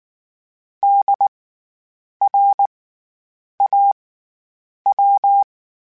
Morse code, heard as a keyed tone, about 19 wpm, spaced as for 8 wpm, 800 Hz. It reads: DRAW
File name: Special INGs / Special INGs-mp3s / SlowToFaster ing @18